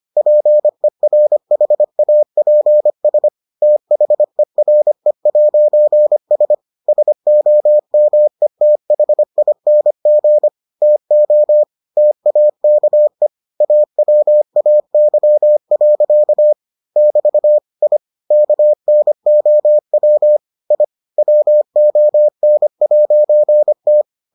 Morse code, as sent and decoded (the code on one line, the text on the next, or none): .--. .-. .... .- .--. ... / - .... .-. .----. ... / ... --- -- . - .... .. -. --. / - --- / - .- -.- . / .- .-- .- -.-- .-.-.- / -...- / .. / -.- -. --- .-- / .. / .-- --- -. .----. -
PERHAPS THERE'S SOMETHING TO TAKE AWAY. <BT> I KNOW I WON'T